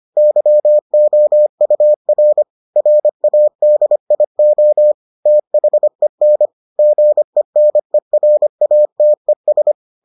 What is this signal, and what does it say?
Morse code, heard as a keyed tone, about 25 wpm, 600 Hz. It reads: YOUR RADIO THEN GENERATES